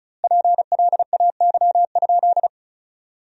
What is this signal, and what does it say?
Morse code, heard as a keyed tone, about 34 wpm, 700 Hz. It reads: PLAY?
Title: 2x Word-Spacing - Sentences from Top 400 Words - 35wpm